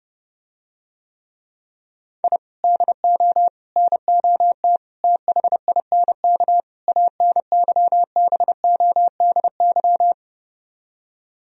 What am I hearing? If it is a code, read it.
Morse code, heard as a keyed tone, about 30 wpm, 700 Hz. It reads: I DO NOT THINK ANYBODY